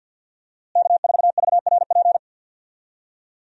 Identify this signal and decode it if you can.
Morse code, heard as a keyed tone, about 50 wpm, 700 Hz. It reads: K4VRP
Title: USA Call Signs - 5 Characters - Repeated 2x - 50wpm